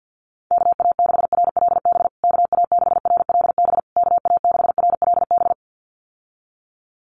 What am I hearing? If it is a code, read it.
Morse code, heard as a keyed tone, about 50 wpm, 700 Hz. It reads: KA6RLB KA6RLB KA6RLB
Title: USA Call Signs Repeated 3x 50wpm